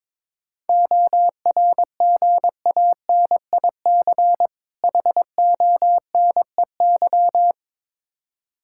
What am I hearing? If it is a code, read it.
Morse code, heard as a keyed tone, about 22 wpm, 700 Hz. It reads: ORGANIC HONEY
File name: Difficult Words with Hints - 22wpm